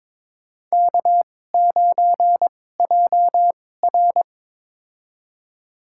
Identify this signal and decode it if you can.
Morse code, heard as a keyed tone, about 22 wpm, spaced as for 16 wpm, 700 Hz. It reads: K9JR
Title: USA Call Signs - 4 Characters - 2x Character Spacing - 22wpm